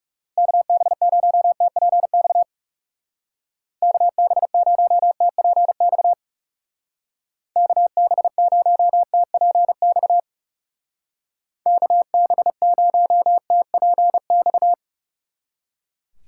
Morse code, heard as a keyed tone, about 37 wpm, 700 Hz. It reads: KB0TPX KB0TPX KB0TPX KB0TPX
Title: Speed Racing - USA 2x3 Call Signs 30wpm